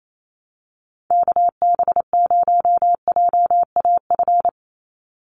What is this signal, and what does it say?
Morse code, heard as a keyed tone, about 28 wpm, 700 Hz. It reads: KB0JAF
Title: USA Call Signs 28wpm